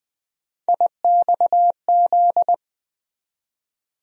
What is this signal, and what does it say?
Morse code, heard as a keyed tone, about 20 wpm, 700 Hz. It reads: IXZ
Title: USA Call Sign Postfixes - 3 Letters 20wpm